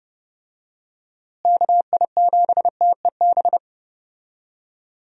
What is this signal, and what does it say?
Morse code, heard as a keyed tone, about 30 wpm, 700 Hz. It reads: KI7TEB
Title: USA Call Signs - 6 Characters - Repeated 3x - 30wpm